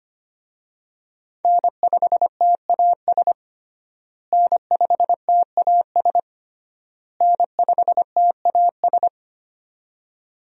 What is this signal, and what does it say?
Morse code, heard as a keyed tone, about 25 wpm, 700 Hz. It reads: N5TAS N5TAS N5TAS
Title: USA Call Signs - 5 Characters - Repeated 3x - 25wpm